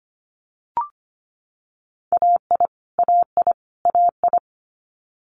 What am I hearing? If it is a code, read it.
Morse code, heard as a keyed tone, about 25 wpm, 700 Hz. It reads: AI AI AI
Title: Multi-letter Phonemes Repeated 3x 25wpm